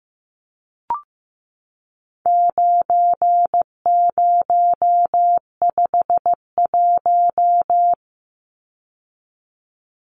Morse code, heard as a keyed tone, about 15 wpm, 700 Hz. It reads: 9051
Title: Numbers - 4-Digit 15wpm